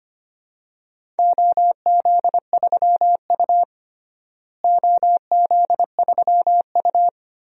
Morse code, heard as a keyed tone, about 25 wpm, 700 Hz. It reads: OZ3U OZ3U
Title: DXCC-100 Call Signs - 4 Characters - Repeated 3x - 25wpm